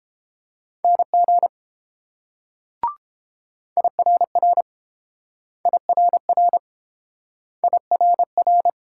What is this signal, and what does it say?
Morse code, heard as a keyed tone, about 30 wpm, 700 Hz. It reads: NG IRR IRR IRR